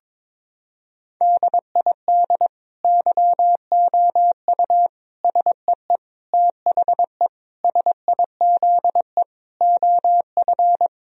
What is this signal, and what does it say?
Morse code, heard as a keyed tone, about 22 wpm, 700 Hz. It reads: DID YOU SEE THE SIZE OF
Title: Sentences from Top 1000 Words - Review Part 2 - 22wpm